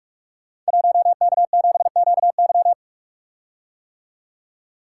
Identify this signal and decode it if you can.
Morse code, heard as a keyed tone, about 44 wpm, 700 Hz. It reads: 1K7QY
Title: Code Groups - 5 Letter:Numbers - 45wpm